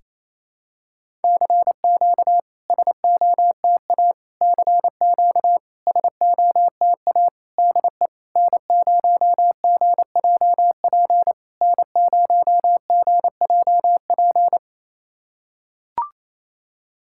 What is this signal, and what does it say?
Morse code, heard as a keyed tone, about 28 wpm, 690 Hz. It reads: CQ SOTA CQ SOTA DE N0GJP N0GJP
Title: SOTA QSO Practice 28wpm